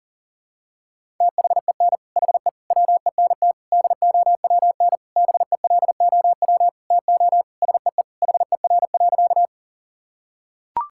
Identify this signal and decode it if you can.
Morse code, heard as a keyed tone, about 37 wpm, 700 Hz. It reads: THEN HE WENT DOWN BELOW TO SEE HER.
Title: Sentences from Top 1000 Words - Review Part 5 - 40wpm